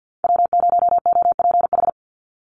Morse code, heard as a keyed tone, about 50 wpm, 700 Hz. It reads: W0OPH
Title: USA Call Signs 50wpm